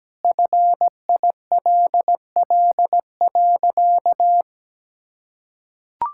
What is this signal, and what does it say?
Morse code, heard as a keyed tone, about 17 wpm, 700 Hz. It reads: FILL.
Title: Sentences from Top 1000 Words - Review Part 5 - 17wpm